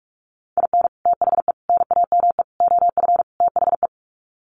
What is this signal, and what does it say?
Morse code, heard as a keyed tone, about 44 wpm, 700 Hz. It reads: IN THE NAME OF THE
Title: Sets of 5 Words Encore 45wpm